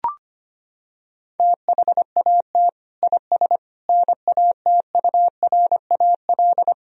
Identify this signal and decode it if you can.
Morse code, heard as a keyed tone, about 25 wpm, 700 Hz. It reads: THAT IS NATURAL
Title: Sentences from Top 1000 Words - Review Part 4 - 25wpm